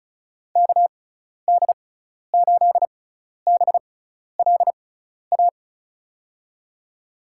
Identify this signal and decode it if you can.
Morse code, heard as a keyed tone, about 36 wpm, spaced as for 12 wpm, 690 Hz. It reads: KD8BLA
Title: USA Call Signs - 6 Characters - 6x Character Spacing - 35wpm